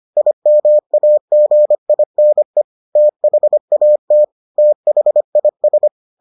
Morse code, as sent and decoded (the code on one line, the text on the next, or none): .. -- .- --. .. -. / - .... .- - / - .... .. ...
IMAGINE THAT THIS